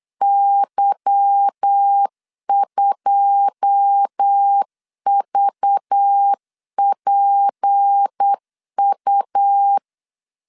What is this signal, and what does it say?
Morse code, heard as a keyed tone, about 8 wpm, 800 Hz. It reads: Y2VPU
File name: LEZIONI / 9 2 8wpm 3437